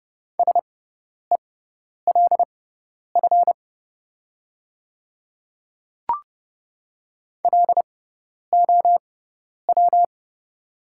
Morse code, heard as a keyed tone, about 30 wpm, spaced as for 10 wpm, 700 Hz. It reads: SELF LOW
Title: 6x Character Spacing - Top 500 Words - 30wpm